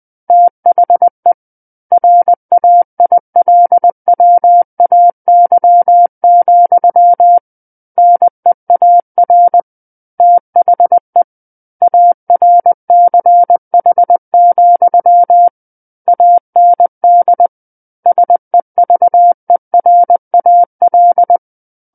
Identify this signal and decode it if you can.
Morse code, heard as a keyed tone, about 20 wpm, 700 Hz. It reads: THE RAILWAY, NEAR THE ARCH, AND SEVERAL
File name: 20 WPM